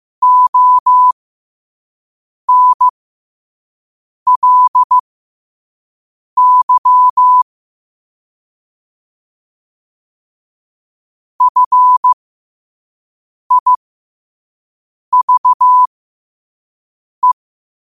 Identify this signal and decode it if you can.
Morse code, heard as a keyed tone, about 15 wpm, spaced as for 5 wpm, 1000 Hz. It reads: ONLY FIVE
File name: qotdfiles / 5wpm